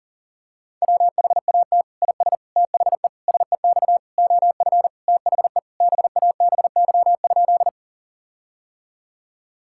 Morse code, heard as a keyed tone, about 38 wpm, 700 Hz. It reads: WHAT IS THE SEX OF THE BABY?
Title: Sentences from Top 1600 Words 40wpm